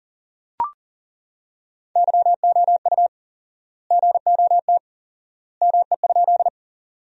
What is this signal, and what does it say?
Morse code, heard as a keyed tone, about 40 wpm, 700 Hz. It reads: YOU GOT ME?
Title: 4x Word-Spacing - Sentences from Top 300 Words - 40wpm